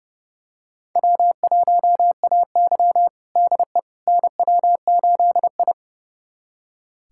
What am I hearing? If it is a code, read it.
Morse code, heard as a keyed tone, about 30 wpm, 700 Hz. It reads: W1AY DE NW8I
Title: USA Callsign DE Callsign - 4 Characters - 30wpm